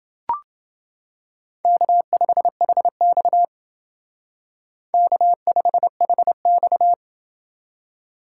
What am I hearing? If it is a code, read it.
Morse code, heard as a keyed tone, about 28 wpm, 700 Hz. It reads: K5HX K5HX